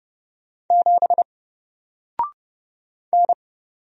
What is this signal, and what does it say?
Morse code, heard as a keyed tone, about 30 wpm, 700 Hz. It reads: 7 N
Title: Single Character (Letter:Number:Punctuation:Prosign) Alt 30wpm